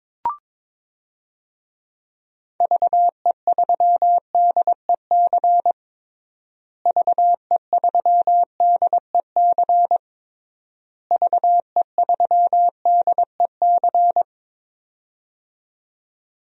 Morse code, heard as a keyed tone, about 23 wpm, 700 Hz. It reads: VE3DEC VE3DEC VE3DEC
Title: DXCC-100 Call Signs - 6 Characters - Repeated 3x - 22wpm